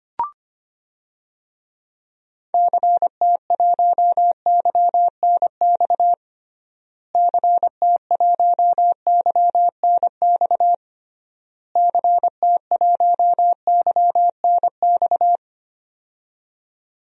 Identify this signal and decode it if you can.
Morse code, heard as a keyed tone, about 25 wpm, 700 Hz. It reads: CT1YNX CT1YNX CT1YNX
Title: DXCC-100 Call Signs - 6 Characters - Repeated 3x - 25wpm